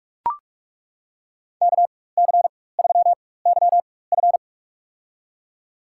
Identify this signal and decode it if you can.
Morse code, heard as a keyed tone, about 48 wpm, spaced as for 21 wpm, 690 Hz. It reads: KC3YF
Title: USA Call Signs - 5 Characters - 4x Character Spacing - 45wpm